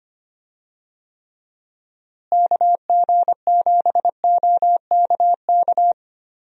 Morse code, heard as a keyed tone, about 25 wpm, 700 Hz. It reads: KG7OKK
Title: USA Call Signs - 6 Characters - Repeated 2x - 25wpm